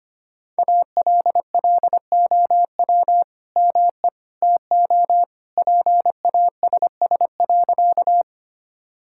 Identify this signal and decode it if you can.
Morse code, heard as a keyed tone, about 25 wpm, 700 Hz. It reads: ALLOW ME TO PASS.
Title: Sentences from Top 1000 Words - Review Part 5 - 25wpm